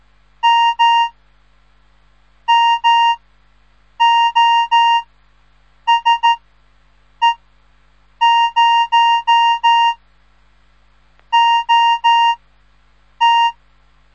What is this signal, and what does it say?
Morse code, heard as a keyed tone, about 13 wpm, 930 Hz. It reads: M M O S E 0 O T
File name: radiomorningstar / morse / les 3